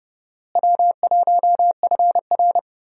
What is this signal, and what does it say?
Morse code, heard as a keyed tone, about 30 wpm, 700 Hz. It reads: W1FR
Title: USA Call Signs - 4 Characters - Repeated 3x - 30wpm